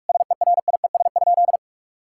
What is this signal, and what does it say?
Morse code, heard as a keyed tone, about 43 wpm, 700 Hz. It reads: SERIES?
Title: Sentences from Top 1000 Words - Review Part 5 - 45wpm